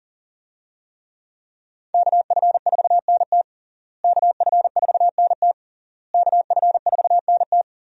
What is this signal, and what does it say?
Morse code, heard as a keyed tone, about 37 wpm, 700 Hz. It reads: KF4NT KF4NT KF4NT
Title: USA Call Signs - 5 Characters - Repeated 3x - 40wpm